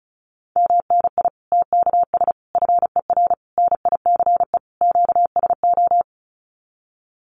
Morse code, heard as a keyed tone, about 34 wpm, 700 Hz. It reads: MNI TKS FER NICE QSO
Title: Common QSO Phrases 35wpm